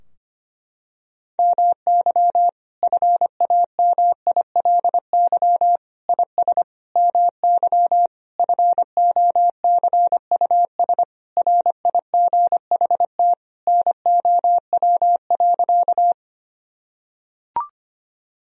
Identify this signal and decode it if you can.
Morse code, heard as a keyed tone, about 25 wpm, 700 Hz. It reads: MY FAMILY IS MY FOCUS RIGHT NOW.
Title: Sentences from Top 1000 Words - Review Part 4 - 25wpm